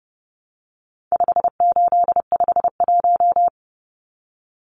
Morse code, heard as a keyed tone, about 30 wpm, 700 Hz. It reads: 5851
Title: Numbers - 4-Digit 30wpm